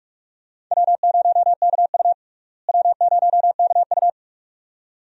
Morse code, heard as a keyed tone, about 42 wpm, 700 Hz. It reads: W0KU W0KU
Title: USA Call Signs - 4 Characters - Repeated 2x - 45wpm